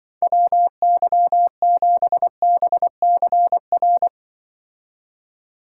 Morse code, heard as a keyed tone, about 25 wpm, 700 Hz. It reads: WY7BCR
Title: Speed Racing - USA 2x3 Call Signs 20wpm